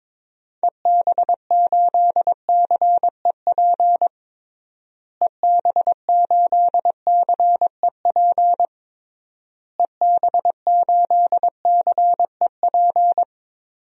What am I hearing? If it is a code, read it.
Morse code, heard as a keyed tone, about 21 wpm, 700 Hz. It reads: EB8CEP EB8CEP EB8CEP
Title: DXCC-100 Call Signs - 6 Characters - Repeated 3x - 22wpm